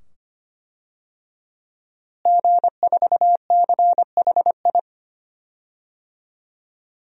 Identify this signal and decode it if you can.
Morse code, heard as a keyed tone, about 25 wpm, 700 Hz. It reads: G4CHI
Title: DXCC-100 Call Signs - 5 Characters - Repeated 2x - 25wpm